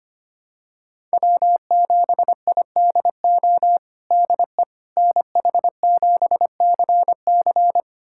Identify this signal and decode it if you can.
Morse code, heard as a keyed tone, about 25 wpm, 700 Hz. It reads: W7IDO DE NH7CC
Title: USA Callsign DE Callsign - 5 Characters - 25wpm